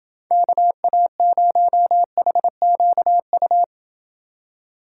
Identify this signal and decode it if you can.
Morse code, heard as a keyed tone, about 29 wpm, 700 Hz. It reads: KA0HQU